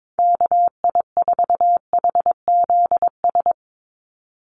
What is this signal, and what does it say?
Morse code, heard as a keyed tone, about 22 wpm, 700 Hz. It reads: KI4HZS
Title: USA Call Signs 22wpm